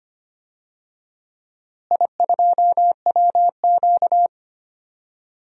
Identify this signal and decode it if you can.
Morse code, heard as a keyed tone, about 25 wpm, 700 Hz. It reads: I2WQ
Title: DXCC-100 Call Signs - 4 Characters - Repeated 2x - 25wpm